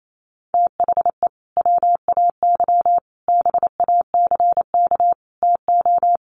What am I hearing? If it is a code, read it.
Morse code, heard as a keyed tone, about 28 wpm, 700 Hz. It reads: THE WAY BACK TO
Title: Sets of 5 Words Encore2 28wpm